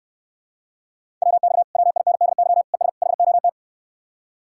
Morse code, heard as a keyed tone, about 69 wpm, 700 Hz. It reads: WX WINDY ES HOT